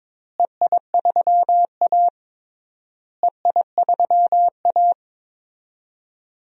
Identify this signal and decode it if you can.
Morse code, heard as a keyed tone, about 22 wpm, 700 Hz. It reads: EI3A EI3A
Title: DXCC-100 Call Signs - 4 Characters - Repeated 3x - 22wpm